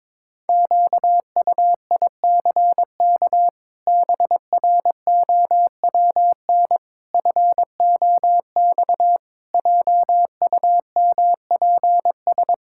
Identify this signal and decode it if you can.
Morse code, heard as a keyed tone, about 22 wpm, 700 Hz. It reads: QUICK BROWN FOX JUMPS
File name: practice / 22wpm / Sets of 8 Words - 22wpm